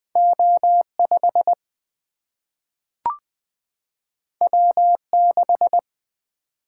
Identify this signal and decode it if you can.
Morse code, heard as a keyed tone, about 20 wpm, 700 Hz. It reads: O 5 W 6
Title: Single-Letter Single-Number 20wpm